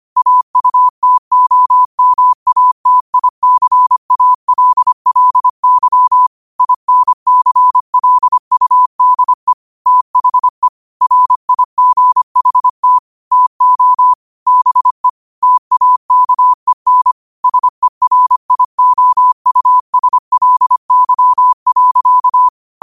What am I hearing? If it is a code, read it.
Morse code, heard as a keyed tone, about 25 wpm, 1000 Hz. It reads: AUTOMATICALLY INCLUDE THE RIGHT TO BE TAKEN SERIOUSLY.
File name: qotdfiles / 25wpm